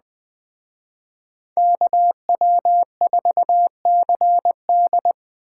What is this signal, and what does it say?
Morse code, heard as a keyed tone, about 21 wpm, 700 Hz. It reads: KW4CD
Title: Speed Racing - USA 2x2 Call Signs 17wpm